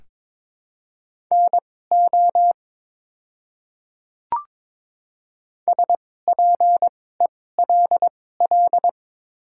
Morse code, heard as a keyed tone, about 22 wpm, spaced as for 16 wpm, 700 Hz. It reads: NO SPELL
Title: 2x Character Spacing - Top 500 Words - 22wpm